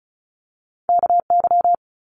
Morse code, heard as a keyed tone, about 34 wpm, 700 Hz. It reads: KY
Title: Multi-letter Phonemes Quick Fire 35wpm